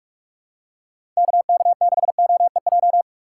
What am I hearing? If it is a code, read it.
Morse code, heard as a keyed tone, about 42 wpm, 700 Hz. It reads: KK6OEJ
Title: USA Call Signs - 6 Characters - 6x Character Spacing - 45wpm